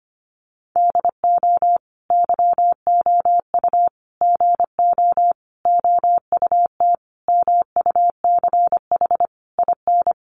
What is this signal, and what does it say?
Morse code, heard as a keyed tone, about 25 wpm, 700 Hz. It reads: DO YOU GO OUT MUCH IN
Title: Sentences from Top 700 Words 25wpm